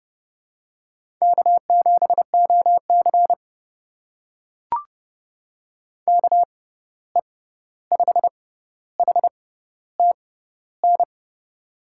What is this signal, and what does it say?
Morse code, heard as a keyed tone, about 31 wpm, spaced as for 14 wpm, 700 Hz. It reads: K7OC KE5HTN
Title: USA Call Signs - 6x Character Spacing - 30wpm